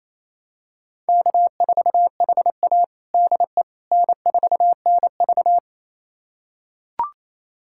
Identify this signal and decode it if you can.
Morse code, heard as a keyed tone, about 28 wpm, 700 Hz. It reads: K4HA DE N4NV
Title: USA Callsign DE Callsign - 4 Characters - 28wpm